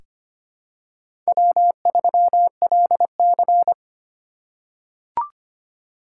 Morse code, heard as a keyed tone, about 25 wpm, 700 Hz. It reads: W3LC
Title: USA Call Signs - 4 Characters - 2x Character Spacing - 25wpm